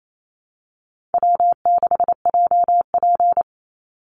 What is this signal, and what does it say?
Morse code, heard as a keyed tone, about 28 wpm, 700 Hz. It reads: W6JP
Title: NA SOTA Chaser Callsigns 28wpm